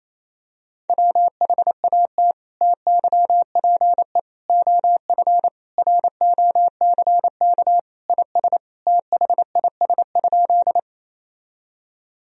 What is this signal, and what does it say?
Morse code, heard as a keyed tone, about 28 wpm, 700 Hz. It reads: WHAT TYPE OF ROCK IS THIS?
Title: Sentences from Top 1000 Words - Review Part 3 - 28wpm